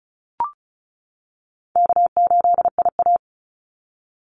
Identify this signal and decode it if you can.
Morse code, heard as a keyed tone, about 34 wpm, 700 Hz. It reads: K8IA
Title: Call Signs - 35wpm